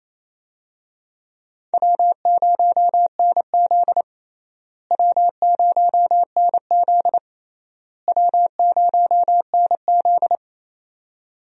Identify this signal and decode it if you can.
Morse code, heard as a keyed tone, about 28 wpm, 700 Hz. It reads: W0NZ W0NZ W0NZ
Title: USA Call Signs - 4 Characters - Repeated 3x - 28wpm